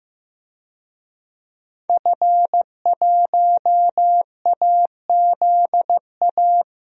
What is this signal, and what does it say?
Morse code, heard as a keyed tone, about 15 wpm, 700 Hz. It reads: F1AZA
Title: DXCC-100 Call Signs - 5 Characters - 15wpm